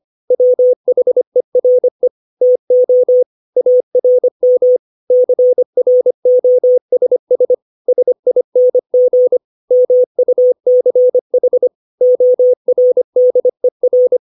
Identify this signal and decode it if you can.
Morse code, heard as a keyed tone, about 25 wpm, 500 Hz. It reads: WHERE TO ARM CROSS SING MUCH ORDER
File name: 632 words / 632 Words 25WPM